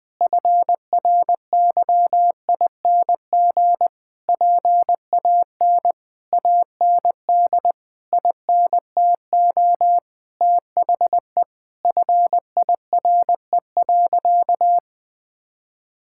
Morse code, heard as a keyed tone, about 20 wpm, 700 Hz. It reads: FRYING PAN AND INTO THE FIRE.
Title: Sentences with 9-Word Phrases - Spoken Beginning - 20wpm